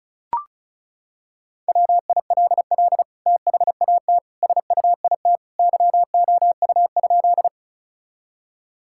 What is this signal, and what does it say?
Morse code, heard as a keyed tone, about 36 wpm, 700 Hz. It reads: WILL THAT SUIT YOU?
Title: Sentences from Top 1600 Words 35wpm